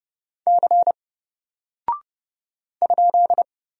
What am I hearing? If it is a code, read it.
Morse code, heard as a keyed tone, about 30 wpm, 700 Hz. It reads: C ?
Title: Single Character (Letter:Number:Punctuation:Prosign) 30wpm